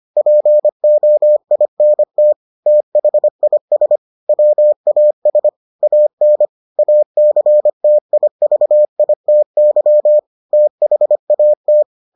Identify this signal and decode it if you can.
Morse code, heard as a keyed tone, about 25 wpm, 600 Hz. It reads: POINT THIS WAS AN ACTIVITY THAT